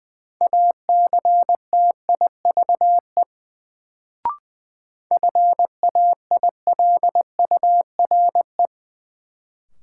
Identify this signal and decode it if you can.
Morse code, heard as a keyed tone, about 20 wpm, 700 Hz. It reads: ACTIVE FAILURE
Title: Sentences from Top 900 Words - New Word Drill - 20wpm